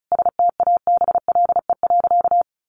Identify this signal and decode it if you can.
Morse code, heard as a keyed tone, about 34 wpm, 700 Hz. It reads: STABLE.
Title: Sentences from Top 1500 Words 35wpm